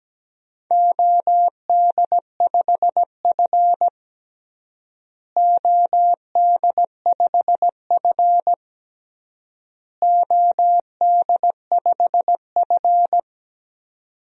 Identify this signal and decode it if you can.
Morse code, heard as a keyed tone, about 17 wpm, 700 Hz. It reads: OD5F OD5F OD5F
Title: DXCC-100 Call Signs - 4 Characters - Repeated 3x - 17wpm